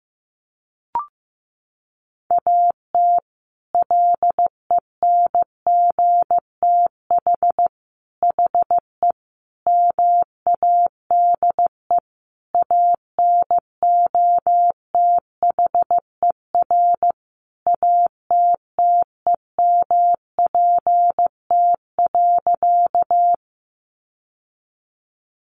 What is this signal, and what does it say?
Morse code, heard as a keyed tone, about 15 wpm, 700 Hz. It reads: AT LENGTH HE MADE ANOTHER ATTEMPT.
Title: Sentences from Top 1000 Words 15wpm